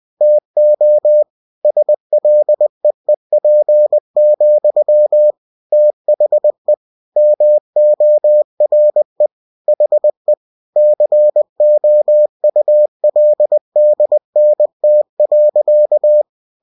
Morse code, heard as a keyed tone, about 20 wpm, 600 Hz. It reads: TO SLEEP, THE MORE HE COULDNT.
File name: cw-books / winnie-the-pooh / winnie-20wpm-page-0039